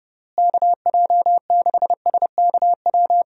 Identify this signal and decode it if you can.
Morse code, heard as a keyed tone, about 31 wpm, 700 Hz. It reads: KJ6SKW
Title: USA Call Signs - 6 Characters - Repeated 3x - 30wpm